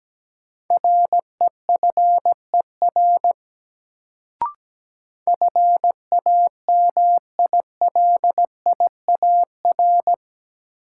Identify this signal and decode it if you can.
Morse code, heard as a keyed tone, about 17 wpm, 700 Hz. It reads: REFER FAMILIAR
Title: Sentences from Top 1000 Words - New Word Drill - 17wpm